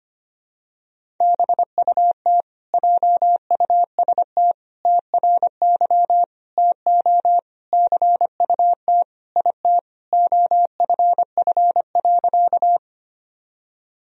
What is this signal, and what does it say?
Morse code, heard as a keyed tone, about 25 wpm, 700 Hz. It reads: BUT JUST TRY TO CUT IT OFF.
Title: Sentences from Top 1000 Words - Review Part 1 - 25wpm